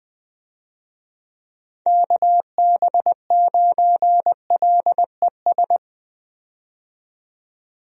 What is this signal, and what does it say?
Morse code, heard as a keyed tone, about 20 wpm, 700 Hz. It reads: KB9LES
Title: USA Call Signs - 6 Characters - 20wpm